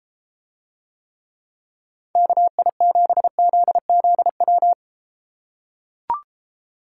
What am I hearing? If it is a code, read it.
Morse code, heard as a keyed tone, about 33 wpm, 700 Hz. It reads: KI7ZZW